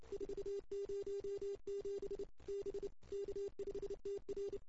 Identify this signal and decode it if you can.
Morse code, heard as a keyed tone, about 29 wpm, 380 Hz. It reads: 407 B K5TR
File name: CW / 2008-ss-cw-k5tr-op-n5rz -00:00